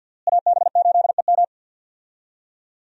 Morse code, heard as a keyed tone, about 47 wpm, 700 Hz. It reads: AB8EM
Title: USA Call Signs - 5 Characters - Repeated 2x - 50wpm